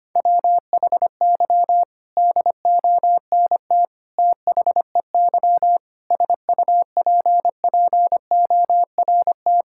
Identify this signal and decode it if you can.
Morse code, heard as a keyed tone, about 25 wpm, 700 Hz. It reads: WHY DONT THEY SUPPORT
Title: Sentences from Top 1000 Words - Review Part 5 - 25wpm